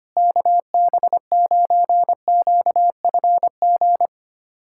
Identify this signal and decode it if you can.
Morse code, heard as a keyed tone, about 25 wpm, 700 Hz. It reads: KB9QFG
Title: USA Callsign DE Callsign - 6 Characters - 25wpm